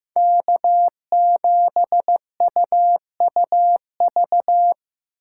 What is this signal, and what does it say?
Morse code, heard as a keyed tone, about 15 wpm, 700 Hz. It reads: K7UUV
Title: USA Call Signs - 5 Characters - 8x Character Spacing - 15wpm